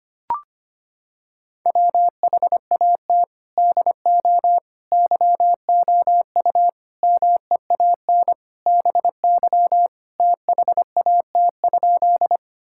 Morse code, heard as a keyed tone, about 25 wpm, 700 Hz. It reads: WHAT DO YOU MEAN BY THAT?
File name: practice / 25wpm / Sentences with 4-Word Phrases - 25wpm